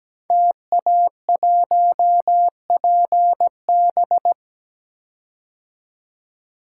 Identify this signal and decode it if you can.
Morse code, heard as a keyed tone, about 17 wpm, 700 Hz. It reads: TA1PB
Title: Code Groups - 5 Letter:Numbers - 17wpm